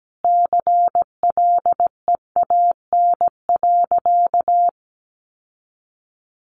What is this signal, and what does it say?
Morse code, heard as a keyed tone, about 17 wpm, 700 Hz. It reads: CLEAN.
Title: Sentences from Top 1100 Words 17wpm